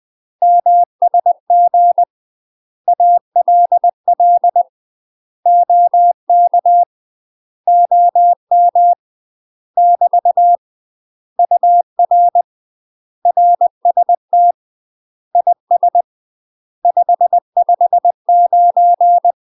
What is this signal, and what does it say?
Morse code, heard as a keyed tone, about 20 wpm, 700 Hz. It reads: MSG ALL OK OM <BT> UR RST IS 559
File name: UK MASTER QSOs TRAINING / 20WPM / UK MASTER QSOs 0070